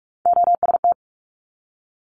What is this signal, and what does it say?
Morse code, heard as a keyed tone, about 46 wpm, 700 Hz. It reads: OST
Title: Multi-letter Phonemes 45wpm